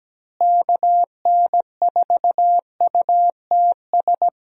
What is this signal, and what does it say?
Morse code, heard as a keyed tone, about 17 wpm, 700 Hz. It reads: KN4UTS
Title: USA Call Signs - 6 Characters - Repeated 3x - 17wpm